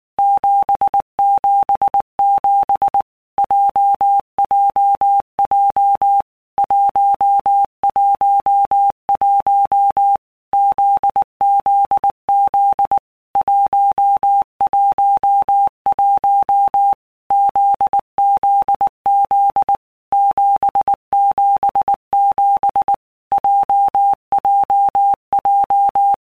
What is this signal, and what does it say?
Morse code, heard as a keyed tone, about 19 wpm, 800 Hz. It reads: 777 JJJ 111 ZZZ 111 ZZZ 777 JJJ
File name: All Groups 1-minute Blocks / Groups@18 / JZ17